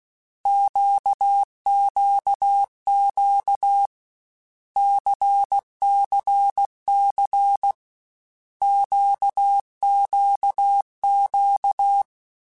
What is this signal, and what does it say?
Morse code, heard as a keyed tone, about 16 wpm, 800 Hz. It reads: QQQ CCC QQQ
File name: Basic Letter Groups By7Groups / LFCYQ / LFCYQ@13